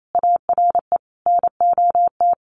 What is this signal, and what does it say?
Morse code, heard as a keyed tone, about 28 wpm, 700 Hz. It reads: ARE NOT
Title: Sentences from Top 1400 Words 28wpm